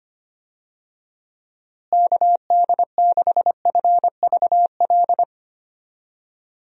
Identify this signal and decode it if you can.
Morse code, heard as a keyed tone, about 25 wpm, 700 Hz. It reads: KD6FVL
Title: USA Call Signs - 6 Characters - Repeated 2x - 25wpm